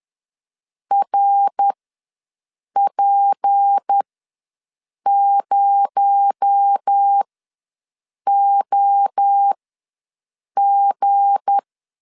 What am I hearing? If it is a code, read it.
Morse code, heard as a keyed tone, about 10 wpm, spaced as for 6 wpm, 800 Hz. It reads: RP0OG
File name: LEZIONI / 7 GF9Z / 7 G ws 10wpm 3843